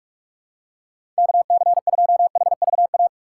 Morse code, heard as a keyed tone, about 43 wpm, 700 Hz. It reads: KC2HVA
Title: USA Call Signs - 6 Characters - 8x Character Spacing - 45wpm